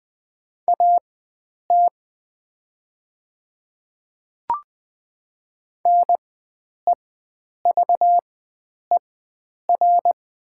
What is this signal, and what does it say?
Morse code, heard as a keyed tone, about 20 wpm, spaced as for 9 wpm, 700 Hz. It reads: AT NEVER
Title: 4x Character Spacing - Top 300 Words - 20wpm